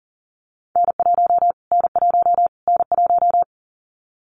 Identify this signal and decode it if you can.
Morse code, heard as a keyed tone, about 40 wpm, 700 Hz. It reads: N1 N1 N1
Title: Callsign Prefixes for Top 25 DXCC - Repeated 3x - 40wpm